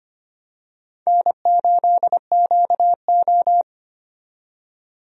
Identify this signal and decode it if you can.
Morse code, heard as a keyed tone, about 25 wpm, 700 Hz. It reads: N8QO
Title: USA Call Signs - 4 Characters - 25wpm